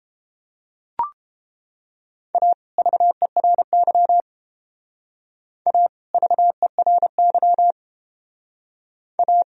Morse code, heard as a keyed tone, about 30 wpm, 700 Hz. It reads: A VERY A VERY A